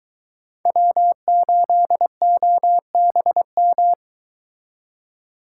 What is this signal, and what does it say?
Morse code, heard as a keyed tone, about 23 wpm, 700 Hz. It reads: W8OBM